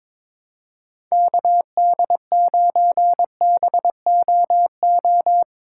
Morse code, heard as a keyed tone, about 22 wpm, 700 Hz. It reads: KD9BOO
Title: USA Call Signs - 6 Characters - 2x Character Spacing - 22wpm